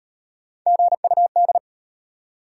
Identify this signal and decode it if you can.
Morse code, heard as a keyed tone, about 33 wpm, 690 Hz. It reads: GUD